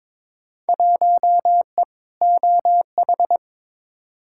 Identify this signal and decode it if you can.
Morse code, heard as a keyed tone, about 23 wpm, 700 Hz. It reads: 1E OH
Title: Field Day Exchanges - 22wpm